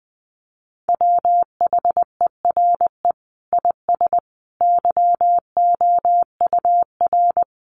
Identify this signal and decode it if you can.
Morse code, heard as a keyed tone, about 19 wpm, 700 Hz. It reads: WHERE IS YOUR
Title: Sentences from Top 1400 Words 20wpm